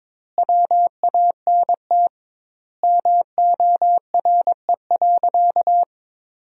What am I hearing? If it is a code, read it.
Morse code, heard as a keyed tone, about 22 wpm, 700 Hz. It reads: WANT MORE.
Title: 2x Word-Spacing - Sentences from Top 200 Words - 22wpm